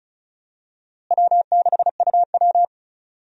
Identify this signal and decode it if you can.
Morse code, heard as a keyed tone, about 36 wpm, 700 Hz. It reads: W6UW
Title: USA Call Signs - 4 Characters - Repeated 2x - 35wpm